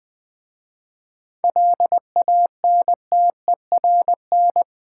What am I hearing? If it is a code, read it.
Morse code, heard as a keyed tone, about 20 wpm, 700 Hz. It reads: LANTERN
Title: Easy Words with Hints - 20wpm